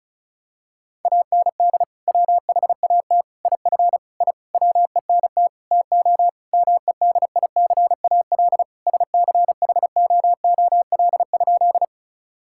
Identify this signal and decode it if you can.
Morse code, heard as a keyed tone, about 34 wpm, 700 Hz. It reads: AND WHAT IF I WENT TO MEDICAL SCHOOL?
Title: Sentences from Top 1000 Words - Review Part 5 - 35wpm